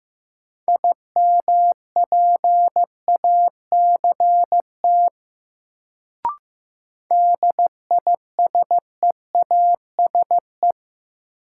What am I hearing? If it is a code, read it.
Morse code, heard as a keyed tone, about 15 wpm, 700 Hz. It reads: IMPACT DISEASE
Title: Sentences from Top 1000 Words - Review Part 6 - New Word Drill - 15wpm